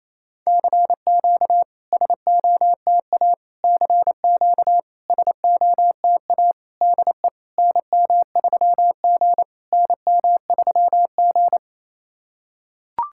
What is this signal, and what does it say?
Morse code, heard as a keyed tone, about 28 wpm, 690 Hz. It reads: CQ SOTA CQ SOTA DE NM3G NM3G
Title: SOTA QSO Practice 28wpm